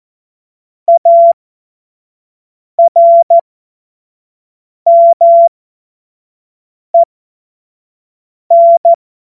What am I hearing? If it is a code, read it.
Morse code, heard as a keyed tone, about 14 wpm, spaced as for 5 wpm, 680 Hz. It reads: ARMEN
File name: Morsekurs / Witze / Witz 30. August 2022 25 F70